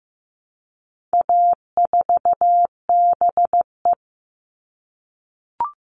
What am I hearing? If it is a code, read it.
Morse code, heard as a keyed tone, about 15 wpm, 700 Hz. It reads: A4BE
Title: All Cut Numbers - 4-Digit 15wpm